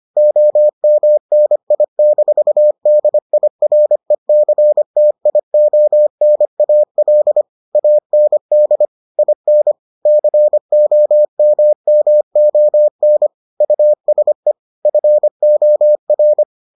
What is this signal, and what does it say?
Morse code, heard as a keyed tone, about 25 wpm, 600 Hz. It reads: OMNI-DIRECTIONAL AND IN COMMON USE FOR